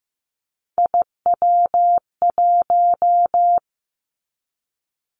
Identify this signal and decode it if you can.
Morse code, heard as a keyed tone, about 15 wpm, 700 Hz. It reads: IW1
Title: Callsign Prefixes for Top 50 DXCC - 15wpm